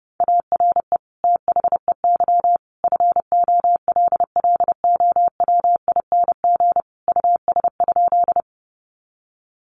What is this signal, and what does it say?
Morse code, heard as a keyed tone, about 28 wpm, 700 Hz. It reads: ARE THEY FOLLOWING US?
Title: Sentences from Top 700 Words 30wpm